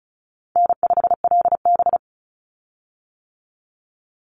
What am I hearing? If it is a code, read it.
Morse code, heard as a keyed tone, about 34 wpm, 700 Hz. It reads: N5LB
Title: Call Signs - 35wpm